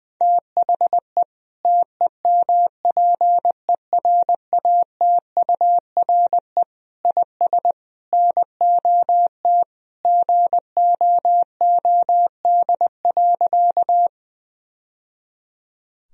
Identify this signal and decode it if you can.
Morse code, heard as a keyed tone, about 20 wpm, 700 Hz. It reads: THE TEMPERATURE IS NOT GOOD.
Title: Sentences from Top 1000 Words - Review Part 4 - 20wpm